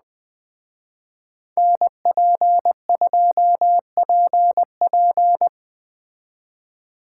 Morse code, heard as a keyed tone, about 20 wpm, 700 Hz. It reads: NP2PP
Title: Speed Racing - USA 2x2 Call Signs 17wpm